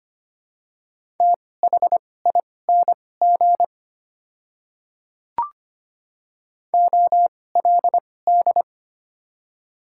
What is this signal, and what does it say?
Morse code, heard as a keyed tone, about 25 wpm, spaced as for 18 wpm, 700 Hz. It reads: THING OLD
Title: 2x Character Spacing - Top 300 Words - 25wpm